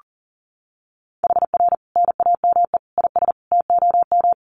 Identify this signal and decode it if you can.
Morse code, heard as a keyed tone, about 40 wpm, 700 Hz. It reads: HR NAME IS TOM
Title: Common QSO Phrases 40wpm